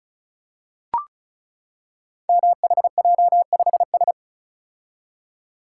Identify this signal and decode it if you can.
Morse code, heard as a keyed tone, about 36 wpm, 700 Hz. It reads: MHJ5S
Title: Code Groups - 5 Letter:Numbers - 35wpm